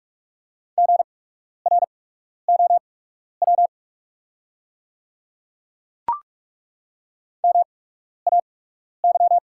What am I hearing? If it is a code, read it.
Morse code, heard as a keyed tone, about 43 wpm, spaced as for 12 wpm, 690 Hz. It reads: GROW MAY
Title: 8x Character Spacing - Top 300 Words - 45wpm